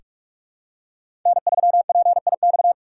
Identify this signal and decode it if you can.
Morse code, heard as a keyed tone, about 47 wpm, 690 Hz. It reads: N3PIX
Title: USA Call Signs - 5 Characters - 6x Character Spacing - 45wpm